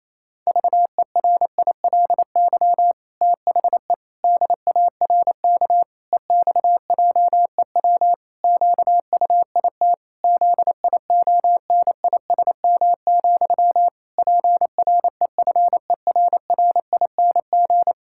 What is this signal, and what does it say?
Morse code, heard as a keyed tone, about 28 wpm, 700 Hz. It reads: VERILY THE DARK EXJEW QUIT ZIONISM, PREFERRING
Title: Pangram Challenge v2 - 28wpm